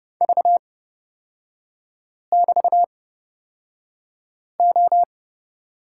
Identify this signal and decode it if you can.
Morse code, heard as a keyed tone, about 31 wpm, spaced as for 5 wpm, 700 Hz. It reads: V<BT>O
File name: Lesson 36e - BT - Three Characters 30wpm